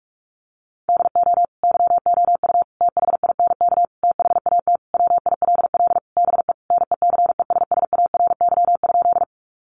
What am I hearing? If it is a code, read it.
Morse code, heard as a keyed tone, about 44 wpm, 700 Hz. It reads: DO YOU THINK THAT WILL BE NECESSARY?
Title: Sentences from Top 800 Words 45wpm